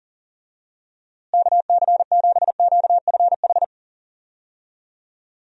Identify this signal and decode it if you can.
Morse code, heard as a keyed tone, about 40 wpm, 700 Hz. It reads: KC7QFH
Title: USA Call Signs - 6 Characters - Repeated 3x - 40wpm